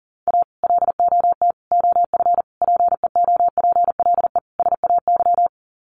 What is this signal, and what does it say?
Morse code, heard as a keyed tone, about 40 wpm, 700 Hz. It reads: A LOT OF PEOPLE SAY
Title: Sets of 5 Words Encore2 40wpm